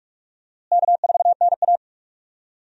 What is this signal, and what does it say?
Morse code, heard as a keyed tone, about 45 wpm, 700 Hz. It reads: K4NA
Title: USA Call Signs - 4 Characters - Repeated 2x - 45wpm